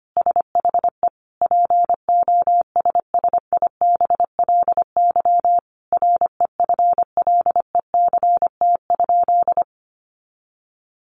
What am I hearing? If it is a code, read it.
Morse code, heard as a keyed tone, about 25 wpm, 700 Hz. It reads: SHE POSSIBLY REFLECT?
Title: Sentences from Top 1500 Words 25wpm